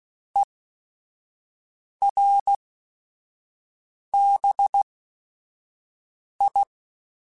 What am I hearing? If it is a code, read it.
Morse code, heard as a keyed tone, about 16 wpm, spaced as for 5 wpm, 800 Hz. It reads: ERBI
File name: QSOs / QSOs-mp3s / WD0BGP 1@5